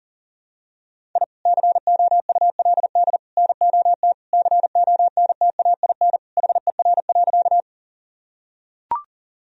Morse code, heard as a keyed tone, about 40 wpm, 700 Hz. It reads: I COULD NOT CONTAIN HER.
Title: Sentences from Top 1000 Words - Review Part 2 - 40wpm